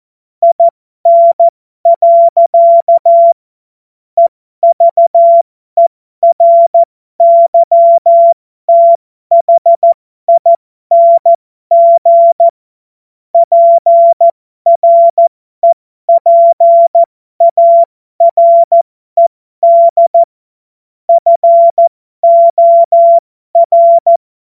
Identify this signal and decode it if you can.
Morse code, heard as a keyed tone, about 14 wpm, spaced as for 12 wpm, 680 Hz. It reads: IN. EVERYTHING PREPARED FOR
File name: Morsekurs / Witze